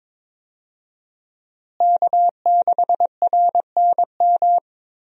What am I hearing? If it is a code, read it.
Morse code, heard as a keyed tone, about 22 wpm, 700 Hz. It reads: K6RNM
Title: USA Call Signs - 5 Characters - Repeated 3x - 22wpm